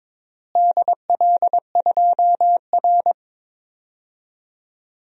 Morse code, heard as a keyed tone, about 22 wpm, 700 Hz. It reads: DL2R
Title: DXCC-100 Call Signs - 4 Characters - Repeated 3x - 22wpm